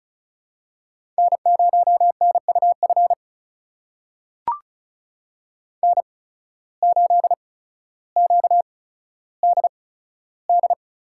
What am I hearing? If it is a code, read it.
Morse code, heard as a keyed tone, about 34 wpm, spaced as for 15 wpm, 700 Hz. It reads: N0NUF N8QDD
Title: USA Call Signs - 5 Characters - 8x Character Spacing - 35wpm